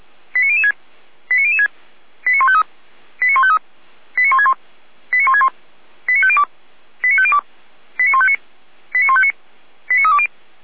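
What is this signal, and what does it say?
Five-tone selective call (ZVEI), heard as quick runs of five tones, tones of about 70 ms. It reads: 89006 89006 89153 89153 89161 89161 89502 89502 89159 89159 89320
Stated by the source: ZVEI five-tone call 89150 received by SDR.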